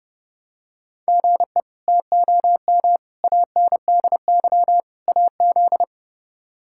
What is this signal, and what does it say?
Morse code, heard as a keyed tone, about 32 wpm, 700 Hz. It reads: GE TOM ANDY AZ
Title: K1USN SST QSO Practice - 30wpm